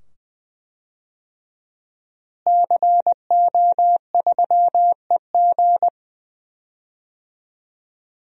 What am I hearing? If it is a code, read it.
Morse code, heard as a keyed tone, about 20 wpm, 700 Hz. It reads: CO3EG
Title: DXCC-100 Call Signs - 5 Characters - Repeated 2x - 20wpm